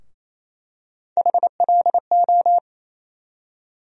Morse code, heard as a keyed tone, about 28 wpm, 700 Hz. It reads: HLO
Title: USA Call Sign Postfixes - 3 Letters 28wpm